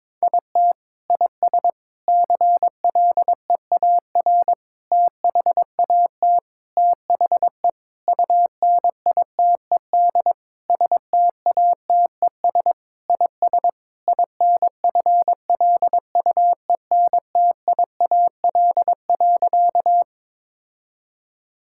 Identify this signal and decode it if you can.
Morse code, heard as a keyed tone, about 22 wpm, 700 Hz. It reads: IT IS CLEAR THAT THE UNITED STATES IS INFLUENTIAL.
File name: practice / 22wpm / Sentences with 4-Word Phrases - 22wpm